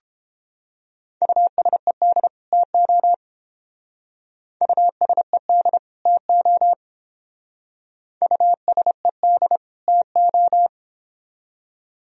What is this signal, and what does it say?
Morse code, heard as a keyed tone, about 28 wpm, 700 Hz. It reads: USED TO USED TO USED TO